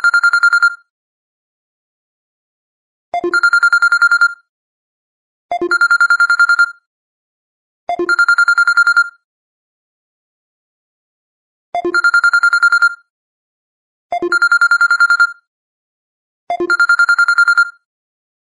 There is no code.